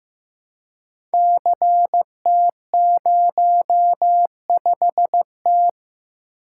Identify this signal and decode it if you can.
Morse code, heard as a keyed tone, about 15 wpm, 700 Hz. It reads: CT05T
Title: Code Groups - 5 Letter:Numbers - 15wpm